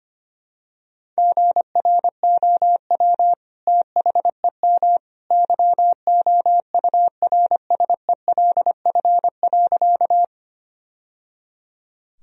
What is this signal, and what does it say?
Morse code, heard as a keyed tone, about 25 wpm, 700 Hz. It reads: GROW THEM YOURSELF.
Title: Sentences from Top 1000 Words - Review Part 1 - 25wpm